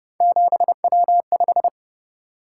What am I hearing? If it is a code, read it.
Morse code, heard as a keyed tone, about 29 wpm, 700 Hz. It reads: 7W5
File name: Speed-Racing - Three Letter-Number Combinations - 25wpm